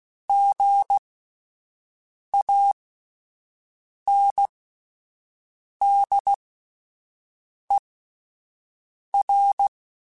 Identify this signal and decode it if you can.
Morse code, heard as a keyed tone, about 16 wpm, spaced as for 5 wpm, 800 Hz. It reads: GANDER
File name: Spaced Alphas / Spaced Alphas mp3s / Gs sp @10